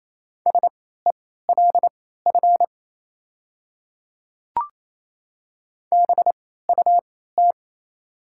Morse code, heard as a keyed tone, about 28 wpm, spaced as for 16 wpm, 700 Hz. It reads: SELF BUT
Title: 3x Character Spacing - Top 300 Words - 28wpm